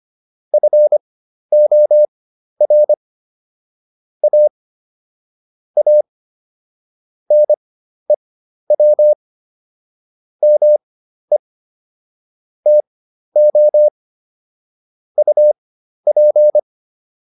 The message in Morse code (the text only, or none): FOR A A NEW ME TO UP